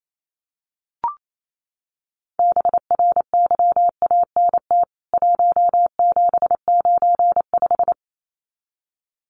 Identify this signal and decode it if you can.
Morse code, heard as a keyed tone, about 28 wpm, 700 Hz. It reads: BRYANT 1795
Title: CWT Exchange 28wpm